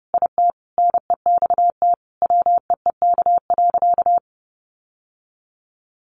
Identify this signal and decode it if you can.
Morse code, heard as a keyed tone, about 30 wpm, 700 Hz. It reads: IT NEXT WEEK.
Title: Sentences from Top 1400 Words 30wpm